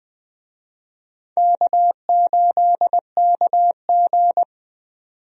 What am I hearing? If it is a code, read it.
Morse code, heard as a keyed tone, about 20 wpm, 700 Hz. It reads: K8KG
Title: USA Call Signs - 4 Characters - 20wpm